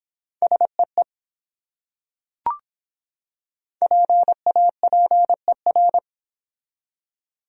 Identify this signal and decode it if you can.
Morse code, heard as a keyed tone, about 24 wpm, 700 Hz. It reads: SEE PAPER